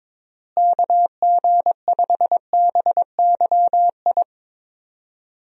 Morse code, heard as a keyed tone, about 22 wpm, 700 Hz. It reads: KG5BYI
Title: USA Call Signs - 6 Characters - Repeated 3x - 22wpm